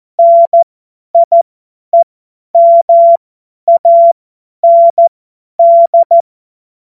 Morse code, heard as a keyed tone, about 14 wpm, spaced as for 10 wpm, 680 Hz. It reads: NIEMAND
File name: Witz-05-August-2025-50-F70